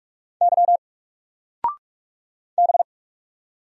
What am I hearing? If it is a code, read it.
Morse code, heard as a keyed tone, about 44 wpm, 700 Hz. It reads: Y B
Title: Single Letter-Number 45wpm